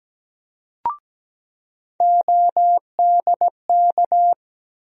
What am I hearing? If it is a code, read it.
Morse code, heard as a keyed tone, about 17 wpm, 700 Hz. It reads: ODK
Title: USA Call Sign Postfixes - 3 Letters 17wpm